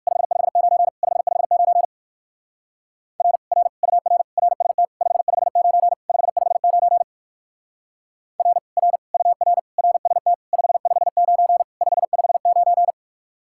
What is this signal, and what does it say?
Morse code, heard as a keyed tone, about 54 wpm, 700 Hz. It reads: 559 559 R R UR RST 559 559 R R UR RST 559 559
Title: Speed-Racing Common QSO Phrases - 45wpm